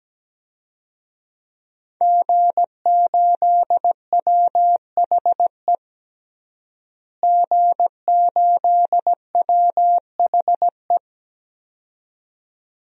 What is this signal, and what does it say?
Morse code, heard as a keyed tone, about 17 wpm, 700 Hz. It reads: G8WHE G8WHE
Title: DXCC-100 Call Signs - 5 Characters - Repeated 2x - 17wpm